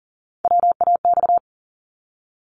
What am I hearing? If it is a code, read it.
Morse code, heard as a keyed tone, about 40 wpm, 700 Hz. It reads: WAX
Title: Multi-letter Phonemes 40wpm